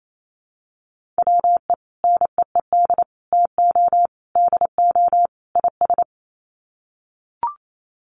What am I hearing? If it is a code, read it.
Morse code, heard as a keyed tone, about 28 wpm, 700 Hz. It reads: WE NEED TO DO IS
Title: Sets of 5 Words Encore2 28wpm